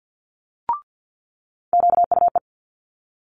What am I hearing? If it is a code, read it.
Morse code, heard as a keyed tone, about 50 wpm, 700 Hz. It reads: QUE
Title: Multi-letter Phonemes Quick Fire 50wpm